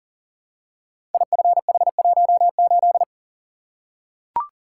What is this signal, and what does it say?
Morse code, heard as a keyed tone, about 39 wpm, 690 Hz. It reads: IFH18
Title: Code Groups - 5 Letter:Numbers - 40wpm